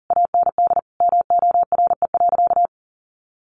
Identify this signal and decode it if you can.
Morse code, heard as a keyed tone, about 40 wpm, 700 Hz. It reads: AND MORE.
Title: Sentences from Top 500 Words 40wpm